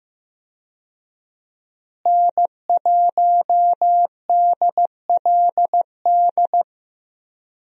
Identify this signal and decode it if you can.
Morse code, heard as a keyed tone, about 15 wpm, 700 Hz. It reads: N1DLD
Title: USA Call Signs - 5 Characters - Repeated 2x - 15wpm